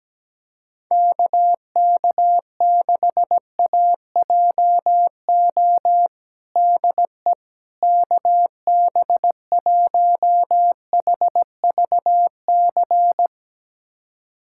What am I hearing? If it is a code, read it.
Morse code, heard as a keyed tone, about 17 wpm, 700 Hz. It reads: KK6AJO DE KB1HVC
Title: USA Callsign DE Callsign 17wpm